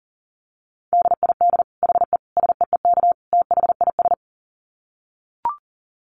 Morse code, heard as a keyed tone, about 40 wpm, 700 Hz. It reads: DID HE SEEK THIS
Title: Dit Challenge 40wpm